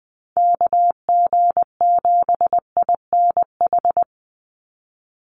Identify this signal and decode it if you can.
Morse code, heard as a keyed tone, about 19 wpm, 700 Hz. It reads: KG7INH
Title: USA Call Signs 20wpm